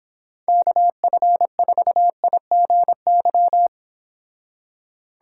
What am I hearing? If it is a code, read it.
Morse code, heard as a keyed tone, about 26 wpm, 700 Hz. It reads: KF4IGY